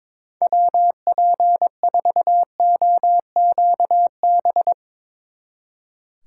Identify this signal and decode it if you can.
Morse code, heard as a keyed tone, about 22 wpm, 700 Hz. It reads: WP4OQB
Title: Speed Racing - USA 2x3 Call Signs 22wpm